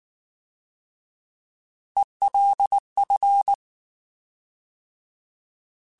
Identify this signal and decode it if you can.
Morse code, heard as a keyed tone, about 19 wpm, 800 Hz. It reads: ELF
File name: Alphabetical / Alphabetical mp3s / Es @18